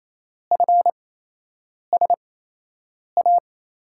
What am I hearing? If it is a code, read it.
Morse code, heard as a keyed tone, about 28 wpm, spaced as for 8 wpm, 700 Hz. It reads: FSA
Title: USA Call Signs - 6 Characters - 8x Character Spacing - 28wpm